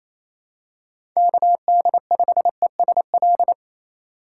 Morse code, heard as a keyed tone, about 28 wpm, 700 Hz. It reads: KD5ESL
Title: USA Call Signs - 6 Characters - 2x Character Spacing - 28wpm